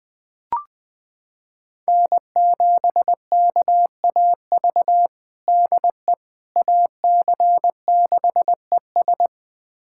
Morse code, heard as a keyed tone, about 20 wpm, 700 Hz. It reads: N7KAV DE AC6ES
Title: USA Callsign DE Callsign - 5 Characters - 20wpm